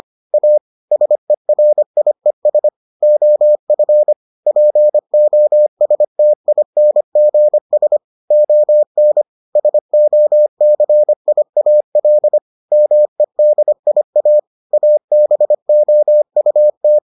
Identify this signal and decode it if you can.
Morse code, heard as a keyed tone, about 25 wpm, 600 Hz. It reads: A SERIES OF POSTINGS ON SOCIAL MEDIA ABOUT